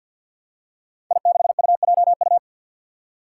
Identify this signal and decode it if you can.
Morse code, heard as a keyed tone, about 53 wpm, 700 Hz. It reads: I6UJU
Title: Five Challenging Letter-Number Combinations 50wpm